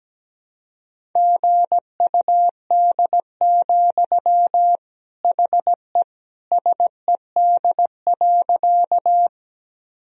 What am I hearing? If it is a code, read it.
Morse code, heard as a keyed tone, about 17 wpm, 700 Hz. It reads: GUD, HE SED.
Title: Sentences from Top 200 Words and Abbreviations 17wpm